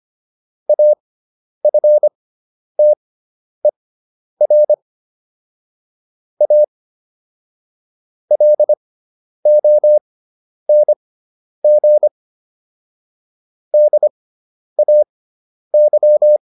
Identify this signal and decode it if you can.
Morse code, heard as a keyed tone, about 25 wpm, spaced as for 10 wpm, 600 Hz. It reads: AFTER A LONG DAY